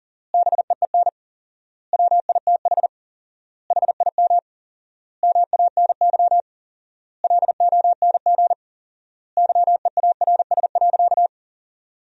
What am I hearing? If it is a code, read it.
Morse code, heard as a keyed tone, about 40 wpm, 700 Hz. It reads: BEEN WITH HIM MANY LONG YEARS.
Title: 4x Word-Spacing - Sentences from Top 200 Words - 40wpm